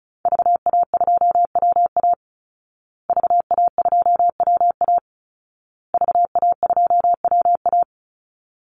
Morse code, heard as a keyed tone, about 34 wpm, 700 Hz. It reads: VA2WA VA2WA VA2WA
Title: Call Signs Repeated 2x - 35wpm